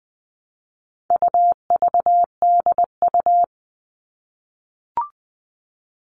Morse code, heard as a keyed tone, about 20 wpm, 700 Hz. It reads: UVDU
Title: All Cut Numbers - 4-Digit 20wpm